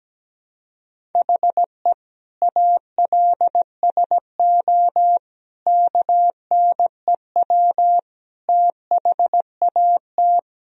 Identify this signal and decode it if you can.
Morse code, heard as a keyed tone, about 17 wpm, 700 Hz. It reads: HE ALSO KNEW THAT
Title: Sentences from Top 1000 Words - Review Part 1 - 17wpm